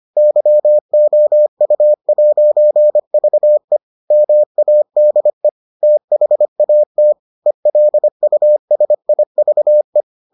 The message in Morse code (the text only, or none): YOU'VE MADE THAT ELUSIVE